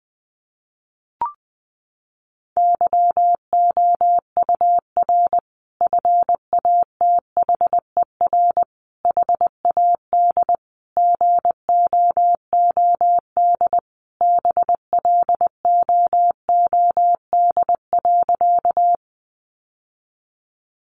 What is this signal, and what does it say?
Morse code, heard as a keyed tone, about 19 wpm, 700 Hz. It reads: YOUR FATHER HAD GOOD BLOOD.
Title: Sentences from Top 1100 Words 20wpm